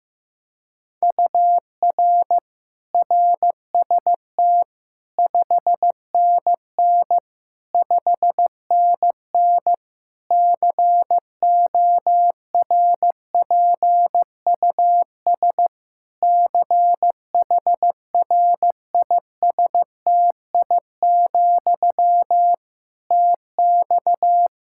Morse code, heard as a keyed tone, about 15 wpm, 700 Hz. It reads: UR RST 5NN 5NN CORPUS CHRISTI, TX
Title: K1USN SST QSO Practice 15wpm